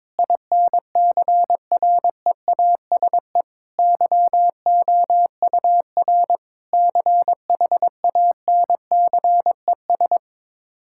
Morse code, heard as a keyed tone, about 22 wpm, 700 Hz. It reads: INCREASE YOUR CHANCES
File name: practice / 22wpm / Sets of 8 Words - 22wpm